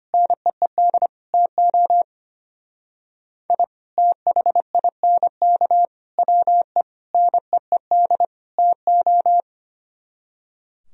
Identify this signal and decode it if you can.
Morse code, heard as a keyed tone, about 31 wpm, 700 Hz. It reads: NEED TO I THINK WE NEED TO
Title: Speed-Racing - Sets of 5 Words 25wpm